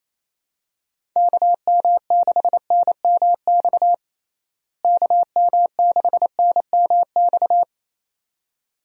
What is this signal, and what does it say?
Morse code, heard as a keyed tone, about 28 wpm, 700 Hz. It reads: KM6NMX KM6NMX
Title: USA Call Signs - 6 Characters - Repeated 2x - 28wpm